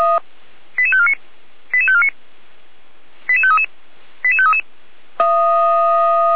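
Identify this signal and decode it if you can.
Five-tone selective call (ZVEI), heard as quick runs of five tones, tones of about 70 ms. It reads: 88549 88549 88533 88533
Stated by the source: ZVEI five-tone call received with an SDR, recorded for Feuerwehr Orlamünde.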